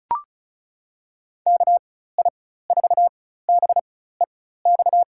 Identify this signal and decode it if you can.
Morse code, heard as a keyed tone, about 36 wpm, spaced as for 16 wpm, 700 Hz. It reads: KI4BEX
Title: USA Call Signs - 6 Characters - 4x Character Spacing - 35wpm